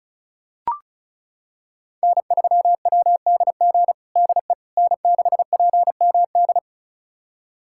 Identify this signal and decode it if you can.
Morse code, heard as a keyed tone, about 36 wpm, 700 Hz. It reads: N3WDG DE N6PMD
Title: USA Callsign DE Callsign - 5 Characters - 35wpm